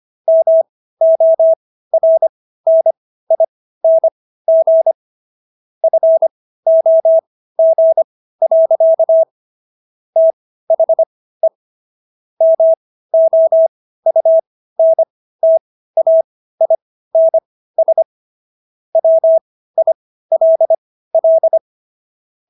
Morse code, heard as a keyed tone, about 25 wpm, spaced as for 15 wpm, 650 Hz. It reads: MORNING FOG. THE MOUNTAINS WILL